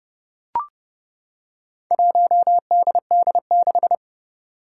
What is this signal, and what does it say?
Morse code, heard as a keyed tone, about 30 wpm, 700 Hz. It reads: 1DD6
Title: Four Challenging Letter-Number Combinations 30wpm